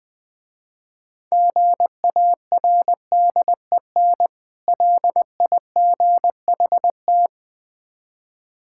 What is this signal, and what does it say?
Morse code, heard as a keyed tone, about 22 wpm, 700 Hz. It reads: GARDEN LIGHT
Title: Difficult Words - Speed-Racing with Hints - 20wpm